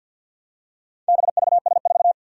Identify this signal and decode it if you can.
Morse code, heard as a keyed tone, about 52 wpm, 700 Hz. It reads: BVS4
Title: Four Challenging Letter-Number Combinations 50wpm